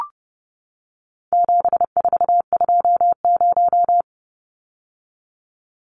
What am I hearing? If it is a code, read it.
Morse code, heard as a keyed tone, about 30 wpm, 700 Hz. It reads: 7420
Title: Numbers - 4-Digit 30wpm